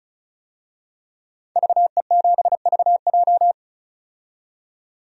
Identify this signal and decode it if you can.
Morse code, heard as a keyed tone, about 34 wpm, 700 Hz. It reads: VE7VJ
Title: DXCC-100 Call Signs - 5 Characters - 35wpm